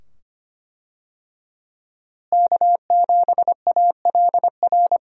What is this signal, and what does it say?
Morse code, heard as a keyed tone, about 25 wpm, 700 Hz. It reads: K7ALR
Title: USA Call Signs - 5 Characters - Repeated 2x - 25wpm